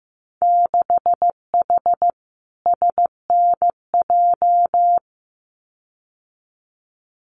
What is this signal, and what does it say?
Morse code, heard as a keyed tone, about 15 wpm, 700 Hz. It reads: 6H SNJ
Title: Winter Field Day Exchanges Encore 15wpm